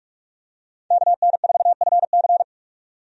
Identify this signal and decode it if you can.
Morse code, heard as a keyed tone, about 42 wpm, 700 Hz. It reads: KN4FC
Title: USA Call Signs - 5 Characters - 45wpm